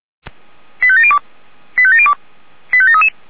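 Five-tone selective call (ZVEI), heard as quick runs of five tones, tones of about 70 ms. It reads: 85702 85702 85733
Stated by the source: ZVEI five-tone selective call 85369 received on an SDR.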